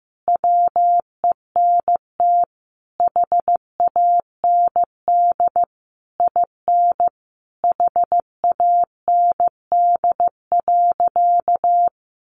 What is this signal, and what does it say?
Morse code, heard as a keyed tone, about 14 wpm, 700 Hz. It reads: WENT HAND IN HAND.
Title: Sentences from Top 1100 Words 15wpm